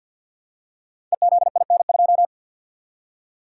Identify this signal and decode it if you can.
Morse code, heard as a keyed tone, about 52 wpm, 700 Hz. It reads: EZIN2
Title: Code Groups - 5 Letter:Numbers - 50wpm